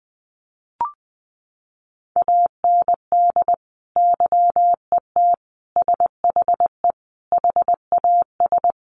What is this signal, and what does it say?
Morse code, heard as a keyed tone, about 20 wpm, 700 Hz. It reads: AND YET SHE HAS
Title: Sentences from Top 1100 Words 20wpm